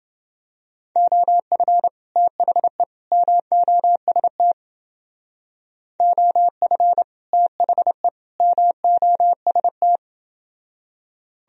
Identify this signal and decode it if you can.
Morse code, heard as a keyed tone, about 29 wpm, 700 Hz. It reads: OF THE MOST OF THE MOST